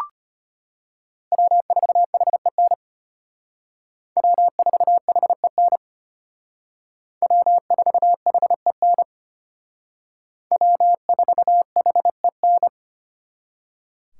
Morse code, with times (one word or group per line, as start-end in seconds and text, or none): 1.3-2.7 W4HEN
4.2-5.8 W4HEN
7.2-9.0 W4HEN
10.5-12.7 W4HEN